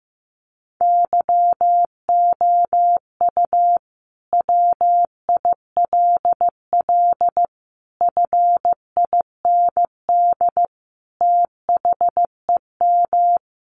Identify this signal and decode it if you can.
Morse code, heard as a keyed tone, about 15 wpm, 700 Hz. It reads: YOU WILL FIND THEM
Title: Sentences from Top 700 Words 15wpm